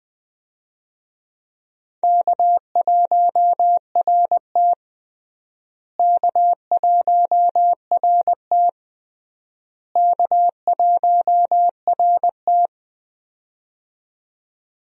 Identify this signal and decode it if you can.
Morse code, heard as a keyed tone, about 20 wpm, 700 Hz. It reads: K1RT K1RT K1RT
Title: USA Call Signs - 4 Characters - Repeated 3x - 20wpm